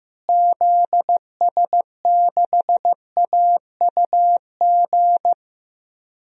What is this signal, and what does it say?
Morse code, heard as a keyed tone, about 15 wpm, 700 Hz. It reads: ZS6AUG
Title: DXCC-100 Call Signs - 6 Characters - Repeated 2x - 15wpm